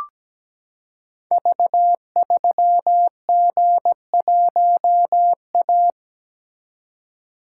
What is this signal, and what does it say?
Morse code, heard as a keyed tone, about 17 wpm, 700 Hz. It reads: V3G1A
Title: Code Groups - 5 Letter:Numbers - 17wpm